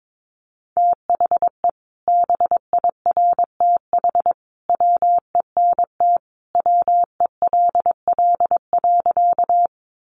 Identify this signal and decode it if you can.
Morse code, heard as a keyed tone, about 22 wpm, 700 Hz. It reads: THE BIRTH WENT WELL.
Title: Sentences from Top 800 Words 22wpm